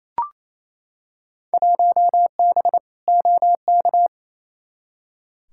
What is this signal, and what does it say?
Morse code, heard as a keyed tone, about 28 wpm, 700 Hz. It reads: 1B OK
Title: Field Day Exchanges Encore - 28wpm